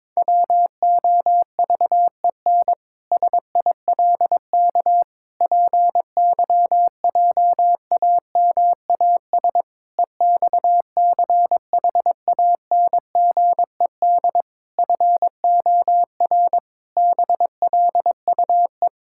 Morse code, heard as a keyed tone, about 22 wpm, 700 Hz. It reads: WOVEN SILK PYJAMAS EXCHANGED FOR BLUE
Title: Pangram Challenge v2 - 22wpm